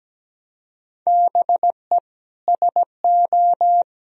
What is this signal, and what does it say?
Morse code, heard as a keyed tone, about 17 wpm, 700 Hz. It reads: BE SO
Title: Sentences from Top 1000 Words - Review Part 1 - 17wpm